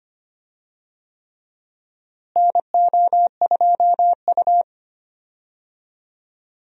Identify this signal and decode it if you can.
Morse code, heard as a keyed tone, about 25 wpm, 700 Hz. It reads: NO2U
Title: USA Call Signs - 4 Characters - Repeated 3x - 25wpm